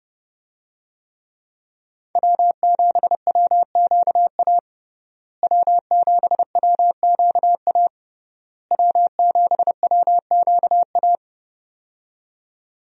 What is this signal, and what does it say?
Morse code, heard as a keyed tone, about 30 wpm, 700 Hz. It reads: W7WQA W7WQA W7WQA
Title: USA Call Signs - 5 Characters - Repeated 3x - 30wpm